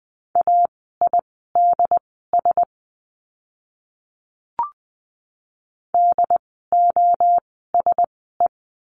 Morse code, heard as a keyed tone, about 20 wpm, spaced as for 14 wpm, 700 Hz. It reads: AIDS DOSE
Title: 068 Lesson 12f - Letter d - Words 20wpm